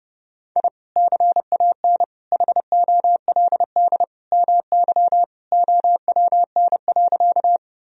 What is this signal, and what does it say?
Morse code, heard as a keyed tone, about 32 wpm, 700 Hz. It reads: I CAN HOLD MY OWN.
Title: Sentences from Top 1000 Words - Review Part 2 - 30wpm